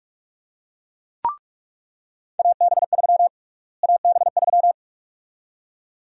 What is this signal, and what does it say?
Morse code, heard as a keyed tone, about 46 wpm, 700 Hz. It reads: AB3 AB3
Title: USA Call Sign Prefixes - Repeated 2x - 45wpm